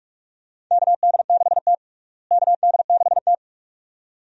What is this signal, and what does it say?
Morse code, heard as a keyed tone, about 46 wpm, 700 Hz. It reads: KD6T KD6T
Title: USA Call Signs - 4 Characters - Repeated 2x - 45wpm